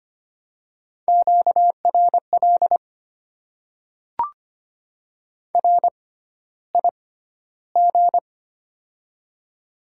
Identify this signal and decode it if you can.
Morse code, heard as a keyed tone, about 25 wpm, spaced as for 13 wpm, 700 Hz. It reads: QRL RIG
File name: Top 20 QSO Elements - 6x Character Spacing - 25wpm